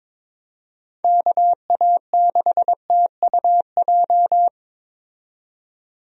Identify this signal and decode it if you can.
Morse code, heard as a keyed tone, about 22 wpm, 700 Hz. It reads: KA6TUJ
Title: USA Call Signs - 6 Characters - 22wpm